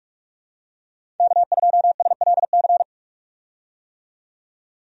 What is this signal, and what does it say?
Morse code, heard as a keyed tone, about 46 wpm, 700 Hz. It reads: K2SLC
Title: USA Call Signs - 5 Characters - 45wpm